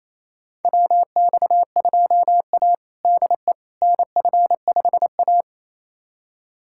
Morse code, heard as a keyed tone, about 28 wpm, 700 Hz. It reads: WX2A DE NF5A
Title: USA Callsign DE Callsign - 4 Characters - 28wpm